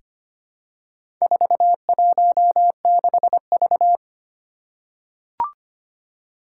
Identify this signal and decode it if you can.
Morse code, heard as a keyed tone, about 25 wpm, 700 Hz. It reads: 416V
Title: Four Challenging Letter-Number Combinations 25wpm